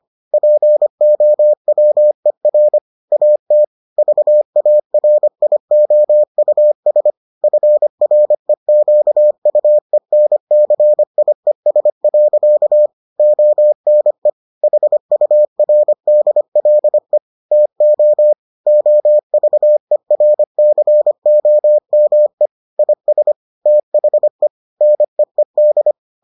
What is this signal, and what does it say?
Morse code, heard as a keyed tone, about 25 wpm, 600 Hz. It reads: POWER AT VARIOUS FREQUENCIES. ONE HURDLE TO OVERCOME IS THE NEED